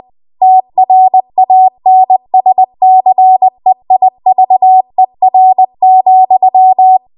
Morse code, heard as a keyed tone, about 20 wpm, 750 Hz. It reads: TRANSCEIVER,
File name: Archive / 20 WPM / 20WPM